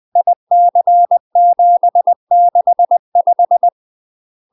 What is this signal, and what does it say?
Morse code, heard as a keyed tone, about 20 wpm, 700 Hz. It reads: IC765
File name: UK MASTER QSOs TRAINING / 20WPM / UK MASTER QSOs 0055